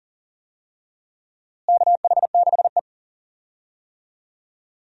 Morse code, heard as a keyed tone, about 40 wpm, 700 Hz. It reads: KH6E
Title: DXCC-100 Call Signs - 4 Characters - Repeated 2x - 40wpm